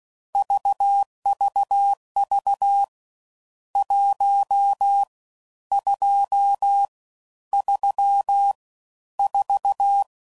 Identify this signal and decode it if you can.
Morse code, heard as a keyed tone, about 16 wpm, 800 Hz. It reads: VVV 1 2 3 4
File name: Mixed / Misc / Numbs